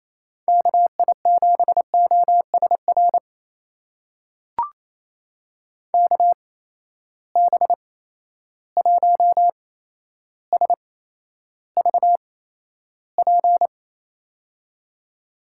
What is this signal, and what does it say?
Morse code, heard as a keyed tone, about 27 wpm, spaced as for 12 wpm, 700 Hz. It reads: KI7OSR KB1SVP
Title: USA Call Signs - 6 Characters - 8x Character Spacing - 28wpm